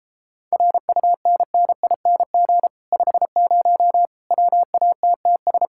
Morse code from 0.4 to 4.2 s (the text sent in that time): RUNNING 50